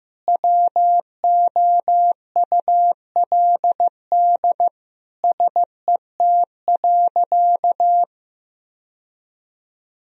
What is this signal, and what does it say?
Morse code, heard as a keyed tone, about 15 wpm, 700 Hz. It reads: WOULD SET.
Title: Sentences from Top 1000 Words - Review Part 1 - 15wpm